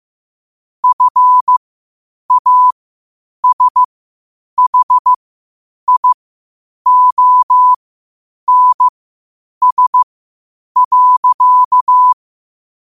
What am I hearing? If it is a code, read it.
Morse code, heard as a keyed tone, about 15 wpm, spaced as for 8 wpm, 1000 Hz. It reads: FASHIONS.
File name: qotdfiles / 8wpm